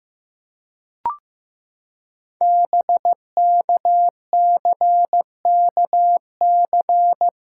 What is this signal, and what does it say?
Morse code, heard as a keyed tone, about 15 wpm, 700 Hz. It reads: BKCKC
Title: Code Groups - 5 Letters - 15wpm